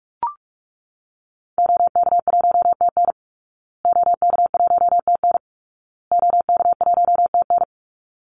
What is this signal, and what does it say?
Morse code, heard as a keyed tone, about 44 wpm, 700 Hz. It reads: OK1TN OK1TN OK1TN
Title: Call Signs Repeated 2x - 45wpm